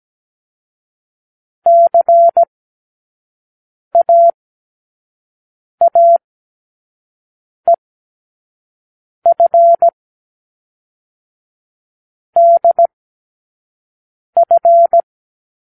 Morse code, heard as a keyed tone, about 17 wpm, 680 Hz. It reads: C A A E F D F
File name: k7qo / Lesson9(A-F)